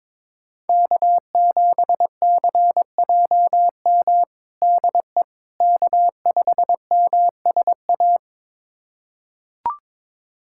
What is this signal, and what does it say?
Morse code, heard as a keyed tone, about 22 wpm, 700 Hz. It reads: K7CJM DE K5MSA
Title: USA Callsign DE Callsign - 5 Characters - 22wpm